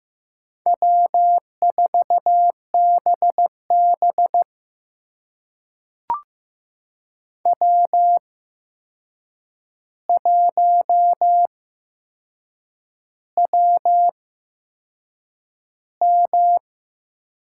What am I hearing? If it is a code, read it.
Morse code, heard as a keyed tone, about 15 wpm, spaced as for 6 wpm, 700 Hz. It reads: W4BB W1WM
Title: USA Call Signs - 8x Character Spacing - 15wpm